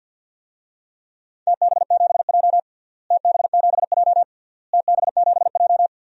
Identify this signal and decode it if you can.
Morse code, heard as a keyed tone, about 51 wpm, 700 Hz. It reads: TB7J TB7J TB7J
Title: DXCC-100 Call Signs - 4 Characters - Repeated 3x - 50wpm